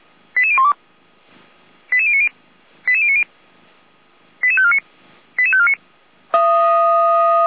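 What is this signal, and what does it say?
Five-tone selective call (ZVEI), heard as quick runs of five tones, tones of about 70 ms. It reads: 88012 88090 88090 88549 88549
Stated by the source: ZVEI five-tone selective call received on SDR.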